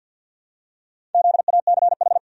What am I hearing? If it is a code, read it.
Morse code, heard as a keyed tone, about 45 wpm, 700 Hz. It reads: ZACH
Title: USA Common Names - Max 5 Letters - 50wpm